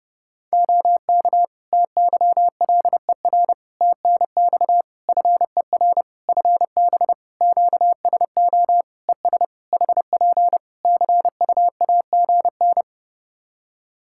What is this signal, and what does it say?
Morse code, heard as a keyed tone, about 32 wpm, 700 Hz. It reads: OK TYLER TNX FER FB QSO ES HP CUAGN
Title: Practice QSO - Exchange 4 - Ending - 30wpm